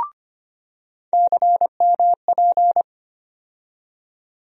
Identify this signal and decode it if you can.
Morse code, heard as a keyed tone, about 25 wpm, 700 Hz. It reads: CMP
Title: USA Call Sign Postfixes - 3 Letters 25wpm